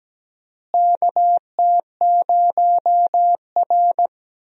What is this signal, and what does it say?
Morse code, heard as a keyed tone, about 17 wpm, 700 Hz. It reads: KT0R
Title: USA Call Signs - 4 Characters - Repeated 3x - 17wpm